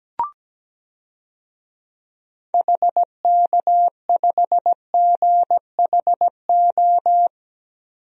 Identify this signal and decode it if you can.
Morse code, heard as a keyed tone, about 17 wpm, 700 Hz. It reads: HK5GHO
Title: DXCC-100 Call Signs - 6 Characters - 17wpm